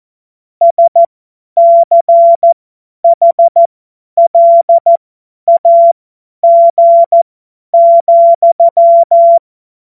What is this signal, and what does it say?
Morse code, heard as a keyed tone, about 14 wpm, spaced as for 10 wpm, 680 Hz. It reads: SCHLAG,
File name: Morsekurs / Witze / Witz 13 Februar 2024 50 F70